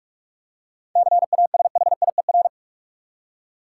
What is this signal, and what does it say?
Morse code, heard as a keyed tone, about 43 wpm, 700 Hz. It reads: CASHIER
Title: Practice QSO - Exchange 3 - More Info - 45wpm